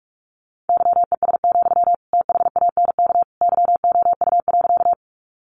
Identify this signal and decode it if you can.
Morse code, heard as a keyed tone, about 44 wpm, 700 Hz. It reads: YES, THANK YOU.
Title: Sentences from Top 600 Words 45wpm